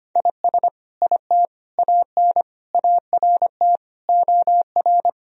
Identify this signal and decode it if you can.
Morse code, heard as a keyed tone, about 25 wpm, 700 Hz. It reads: IS IT AN ART OR
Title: Sentences from Top 1000 Words - Review Part 3 - 25wpm